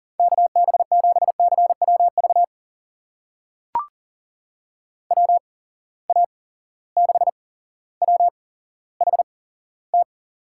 Morse code, heard as a keyed tone, about 40 wpm, spaced as for 17 wpm, 700 Hz. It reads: KB7CWV WA6WHT
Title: USA Call Signs - 6 Characters - 8x Character Spacing - 40wpm